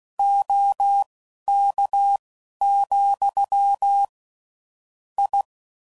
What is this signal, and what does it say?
Morse code, heard as a keyed tone, about 16 wpm, spaced as for 11 wpm, 800 Hz. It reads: OK, I
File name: CW Code Groups @10WPM / QSOs / QSO kc7sac